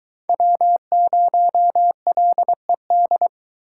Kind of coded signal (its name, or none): Morse code